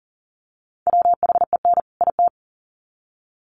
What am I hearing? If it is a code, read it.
Morse code, heard as a keyed tone, about 40 wpm, 700 Hz. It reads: WHEN IT
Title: Sets of 2 Words 40wpm